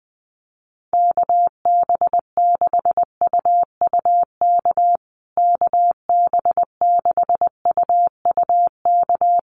Morse code, heard as a keyed tone, about 20 wpm, 700 Hz. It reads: KB6UUK KB6UUK
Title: USA Call Signs Repeated 3x 20wpm